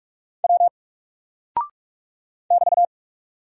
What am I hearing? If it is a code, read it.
Morse code, heard as a keyed tone, about 46 wpm, 700 Hz. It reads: W <BT>
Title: Single Character (Letter:Number:Punctuation:Prosign) 45wpm